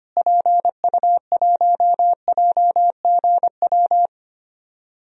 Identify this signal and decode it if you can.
Morse code, heard as a keyed tone, about 25 wpm, 700 Hz. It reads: PU1JGW
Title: DXCC-100 Call Signs - 6 Characters - Repeated 2x - 25wpm